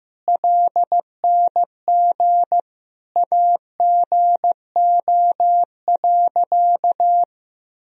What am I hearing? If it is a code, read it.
Morse code, heard as a keyed tone, about 15 wpm, 700 Hz. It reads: LNG AGO.
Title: Sentences from Top 500 Words and Abbreviations 15wpm